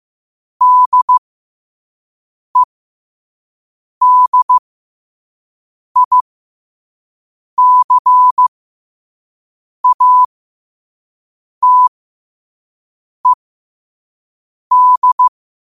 Morse code, heard as a keyed tone, about 15 wpm, spaced as for 5 wpm, 1000 Hz. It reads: DEDICATED
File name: qotdfiles / 5wpm